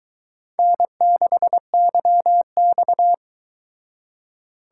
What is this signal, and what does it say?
Morse code, heard as a keyed tone, about 23 wpm, 700 Hz. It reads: N6YX